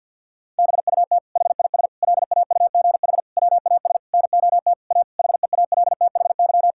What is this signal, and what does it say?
Morse code, heard as a keyed tone, about 52 wpm, 700 Hz. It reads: BUT HIS LAUGH WAS NOT A HEALTHY
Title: Sentences from Top 1000 Words - Review Part 5 - 50wpm